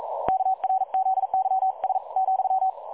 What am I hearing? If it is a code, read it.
Morse code, heard as a keyed tone, about 42 wpm, spaced as for 37 wpm, 760 Hz. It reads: KRZYS,
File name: dlazaawansowanych / nagraniazpasma / websdr recording 3565.0kHz